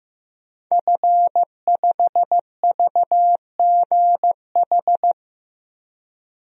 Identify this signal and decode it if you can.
Morse code, heard as a keyed tone, about 15 wpm, 700 Hz. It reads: F5VGH
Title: DXCC-100 Call Signs - 5 Characters - Repeated 3x - 15wpm